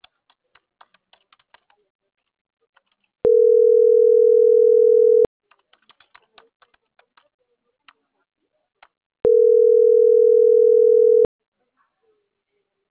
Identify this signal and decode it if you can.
Ringback tone (North America), 440 + 480 Hz.